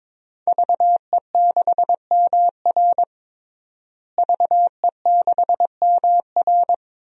Morse code, heard as a keyed tone, about 22 wpm, 700 Hz. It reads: VE6MR VE6MR
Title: DXCC-100 Call Signs - 5 Characters - Repeated 3x - 22wpm